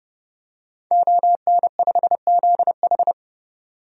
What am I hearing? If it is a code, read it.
Morse code, heard as a keyed tone, about 30 wpm, 700 Hz. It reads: ON5ZH
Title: DXCC-100 Call Signs - 5 Characters - 30wpm